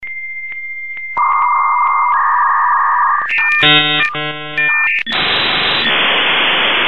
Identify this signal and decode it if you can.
Dial-up modem handshake.
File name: ringtones / 56k-Dialup-Modem